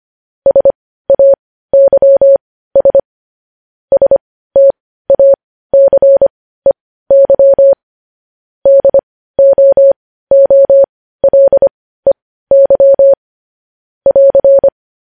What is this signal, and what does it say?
Morse code, heard as a keyed tone, about 25 wpm, spaced as for 15 wpm, 550 Hz. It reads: SAYS STACEY DOOLEY <AR>
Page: news-headlines-in-morse-code-at-15-wpm